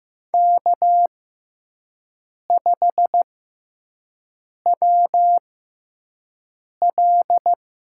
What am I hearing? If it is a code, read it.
Morse code, heard as a keyed tone, about 15 wpm, spaced as for 5 wpm, 700 Hz. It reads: K5WL
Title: USA Call Signs - 5 Characters - 6x Character Spacing - 15wpm